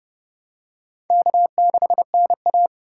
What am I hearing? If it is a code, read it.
Morse code, heard as a keyed tone, about 30 wpm, 700 Hz. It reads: K6NA
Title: K1USN SST QSO Practice - 30wpm